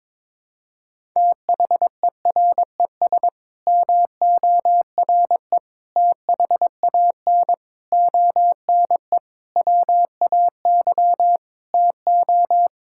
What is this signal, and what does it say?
Morse code, heard as a keyed tone, about 22 wpm, 700 Hz. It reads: THERES MORE THAN ONE WAY TO
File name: Sets of 9 Words - 22wpm